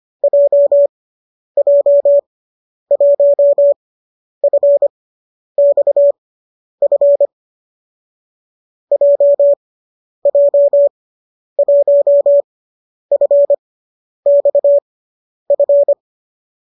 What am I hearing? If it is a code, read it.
Morse code, heard as a keyed tone, about 25 wpm, spaced as for 10 wpm, 570 Hz. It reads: JJ1FXF JJ1FXF
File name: qso005 10